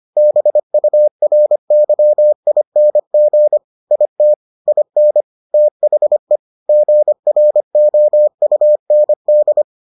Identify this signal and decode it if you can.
Morse code, heard as a keyed tone, about 25 wpm, 600 Hz. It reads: BURYING IT IN THE GROUND